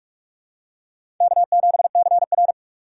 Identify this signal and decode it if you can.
Morse code, heard as a keyed tone, about 44 wpm, 700 Hz. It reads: K7CR
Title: USA Call Signs - 4 Characters - 3x Character Spacing - 45wpm